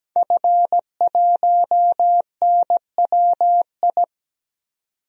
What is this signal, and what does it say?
Morse code, heard as a keyed tone, about 17 wpm, 700 Hz. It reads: F1NWI
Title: DXCC-100 Call Signs - 5 Characters - 17wpm